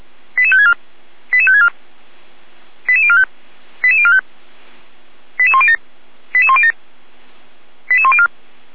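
Five-tone selective call (ZVEI), heard as quick runs of five tones, tones of about 70 ms. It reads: 88564 88564 88045 88045 88107 88107 88104
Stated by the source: ZVEI five-tone call received by SDR, from audio.f85360.feuerwehr-orlamuende.org.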